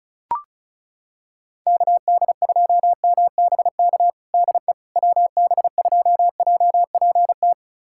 Morse code, heard as a keyed tone, about 36 wpm, 700 Hz. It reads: KD2MBK DE WB2JPT
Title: USA Callsign DE Callsign 35wpm